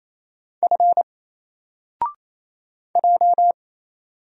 Morse code, heard as a keyed tone, about 28 wpm, 700 Hz. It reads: F J
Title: Single Letter-Number Alt 28wpm